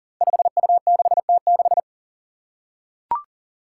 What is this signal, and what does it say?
Morse code, heard as a keyed tone, about 40 wpm, 700 Hz. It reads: 5U6T6
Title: Five Challenging Letter-Number Combinations 40wpm